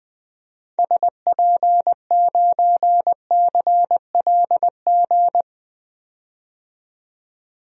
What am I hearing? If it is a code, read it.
Morse code, heard as a keyed tone, about 20 wpm, 700 Hz. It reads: SP9CLG
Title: DXCC-100 Call Signs - 6 Characters - 20wpm